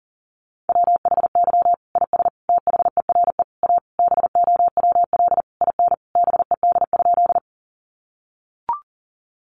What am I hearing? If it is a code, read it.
Morse code, heard as a keyed tone, about 40 wpm, 700 Hz. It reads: WHY IS THERE A BOWL IN BED?
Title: Sentences from Top 1200 Words 40wpm